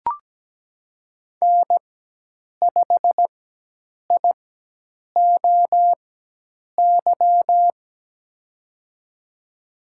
Morse code, heard as a keyed tone, about 17 wpm, spaced as for 8 wpm, 700 Hz. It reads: N5IOY
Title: USA Call Signs - 5 Characters - 4x Character Spacing - 17wpm